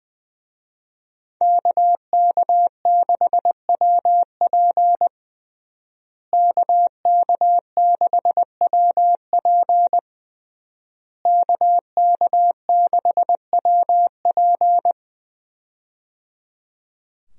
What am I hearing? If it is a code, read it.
Morse code, heard as a keyed tone, about 20 wpm, 700 Hz. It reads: KK6WP KK6WP KK6WP
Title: USA Call Signs - 5 Characters - Repeated 3x - 20wpm